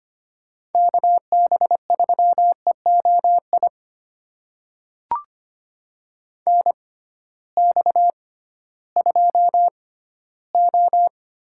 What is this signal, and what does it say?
Morse code, heard as a keyed tone, about 25 wpm, spaced as for 15 wpm, 700 Hz. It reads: KB3EOI NX2O
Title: USA Call Signs - 6x Character Spacing - 25wpm